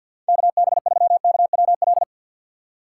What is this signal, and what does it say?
Morse code, heard as a keyed tone, about 50 wpm, 700 Hz. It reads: KB3KWL
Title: USA Call Signs - 6 Characters - 3x Character Spacing - 50wpm